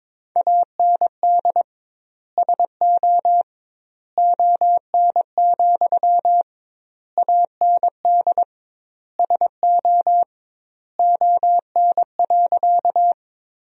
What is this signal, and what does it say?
Morse code, heard as a keyed tone, about 21 wpm, 700 Hz. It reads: AND SO ON, AND SO ON.
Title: 2x Word-Spacing - Sentences from Top 100 Words - 22wpm